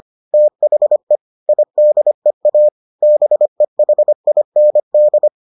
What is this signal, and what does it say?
Morse code, heard as a keyed tone, about 25 wpm, 600 Hz. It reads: THE IDEA BEHIND